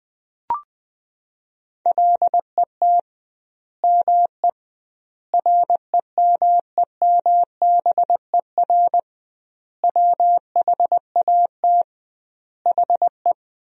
Morse code, heard as a keyed tone, about 21 wpm, 700 Hz. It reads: LET ME REMEMBER WHAT HE
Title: 2x Word-Spacing - Sentences from Top 400 Words - 20wpm